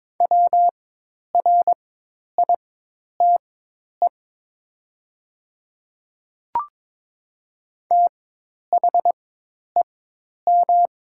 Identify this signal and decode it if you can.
Morse code, heard as a keyed tone, about 22 wpm, spaced as for 10 wpm, 700 Hz. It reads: WRITE THEM
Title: 4x Character Spacing - Top 100 Words - 22wpm